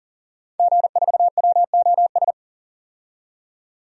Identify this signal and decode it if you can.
Morse code, heard as a keyed tone, about 40 wpm, 700 Hz. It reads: G4WOS
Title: DXCC-100 Call Signs - 5 Characters - 40wpm